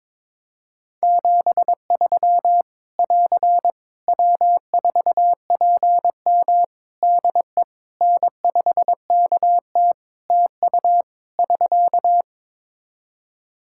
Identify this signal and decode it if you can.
Morse code, heard as a keyed tone, about 22 wpm, 700 Hz. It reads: 73 <AR> W4PM DE N5KT TU <SK>
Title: K1USN SST QSO Practice 22wpm